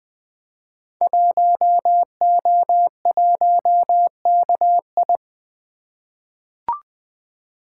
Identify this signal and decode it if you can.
Morse code, heard as a keyed tone, about 20 wpm, 700 Hz. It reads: 1O1KI
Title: Code Groups - 5 Letter:Numbers - 20wpm